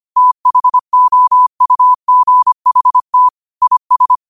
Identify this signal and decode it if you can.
Morse code, heard as a keyed tone, about 25 wpm, 1000 Hz. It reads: THOUGHT IS
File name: qotdfiles / 25wpm